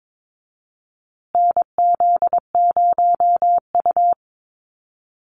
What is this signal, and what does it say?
Morse code, heard as a keyed tone, about 22 wpm, 700 Hz. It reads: NZ0U
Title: USA Call Signs Repeated 3x 22wpm